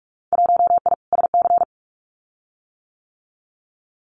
Morse code, heard as a keyed tone, about 46 wpm, 700 Hz. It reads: 1I SC
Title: Winter Field Day Exchanges Encore 45wpm